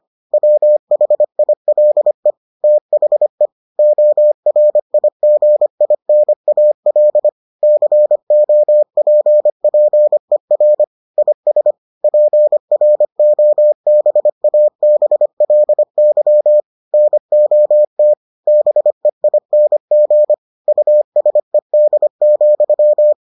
Morse code, heard as a keyed tone, about 25 wpm, 600 Hz. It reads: WHILE THE ORIGINAL COPPER IS PROBABLY NOT BEING USED,